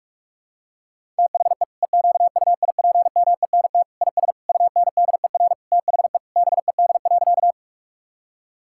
Morse code, heard as a keyed tone, about 42 wpm, 700 Hz. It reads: THE EQUIPMENT IS UNDER THE BED.
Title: Sentences from Top 1000 Words - Review Part 5 - 45wpm